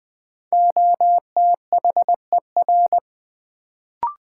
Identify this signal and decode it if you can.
Morse code, heard as a keyed tone, about 20 wpm, 700 Hz. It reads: OTHER
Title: Sentences from Top 100 Words - New Word Drill - 20wpm